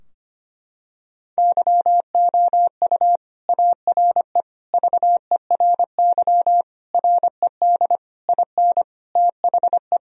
Morse code, heard as a keyed tone, about 25 wpm, 700 Hz. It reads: YOU ARE VERY RED IN THE
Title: Sentences from Top 1000 Words - Review Part 2 - 25wpm